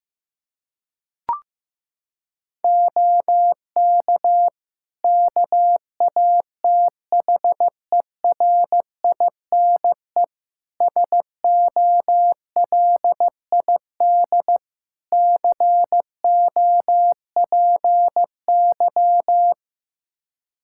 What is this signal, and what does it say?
Morse code, heard as a keyed tone, about 16 wpm, 700 Hz. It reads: OK KATHERINE SOLID COPY
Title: Practice QSO - Exchange 3 - More Info - 15wpm